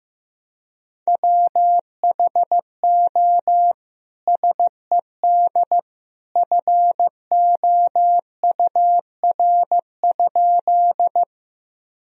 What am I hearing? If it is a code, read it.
Morse code, heard as a keyed tone, about 15 wpm, 700 Hz. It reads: WHO SED FOUR?
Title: Sentences from Top 300 Words and Abbreviations 15wpm